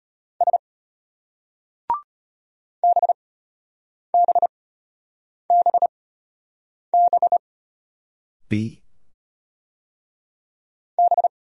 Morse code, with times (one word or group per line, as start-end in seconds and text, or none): 0.4-0.6 S
2.8-7.4 BBBB
11.0-11.3 B